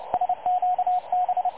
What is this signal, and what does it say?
Morse code, heard as a keyed tone, about 29 wpm, 730 Hz, with radio noise. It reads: SQB
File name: dlazaawansowanych / nagraniazpasma / websdr recording 3565.0kHz